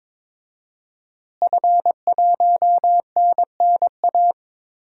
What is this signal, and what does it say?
Morse code, heard as a keyed tone, about 22 wpm, 700 Hz. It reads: F1NNA
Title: DXCC-100 Call Signs - 5 Characters - Repeated 3x - 22wpm